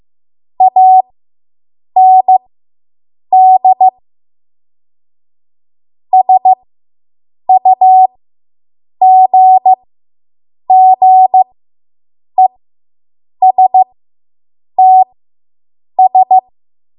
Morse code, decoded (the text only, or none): AND SUGGESTS